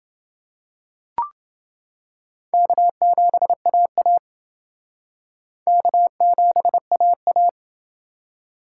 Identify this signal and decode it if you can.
Morse code, heard as a keyed tone, about 28 wpm, 700 Hz. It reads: K7AA K7AA